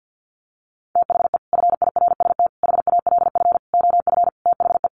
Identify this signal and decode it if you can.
Morse code, heard as a keyed tone, about 50 wpm, 700 Hz. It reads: THE FIRST HALF OF THE
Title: Sets of 5 Words 50wpm